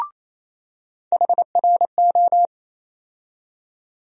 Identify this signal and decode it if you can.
Morse code, heard as a keyed tone, about 28 wpm, 700 Hz. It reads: HRO
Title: USA Call Sign Postfixes - 3 Letters 28wpm